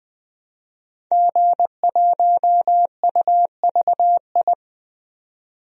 Morse code, heard as a keyed tone, about 20 wpm, 700 Hz. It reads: G1UVI
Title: DXCC-100 Call Signs - 5 Characters - 20wpm